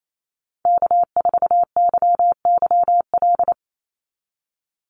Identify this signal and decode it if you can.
Morse code, heard as a keyed tone, about 28 wpm, 700 Hz. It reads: K4YYL
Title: Call Signs Repeated 2x - 28wpm